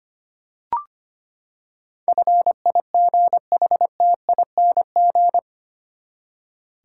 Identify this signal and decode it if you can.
Morse code, heard as a keyed tone, about 25 wpm, 700 Hz. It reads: FIGHTING
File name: Eight Letter Words - 25wpm